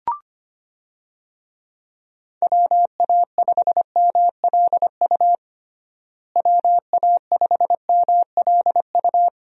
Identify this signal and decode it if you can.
Morse code, heard as a keyed tone, about 25 wpm, 700 Hz. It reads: WA5MLU WA5MLU
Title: USA Call Signs - 6 Characters - Repeated 3x - 25wpm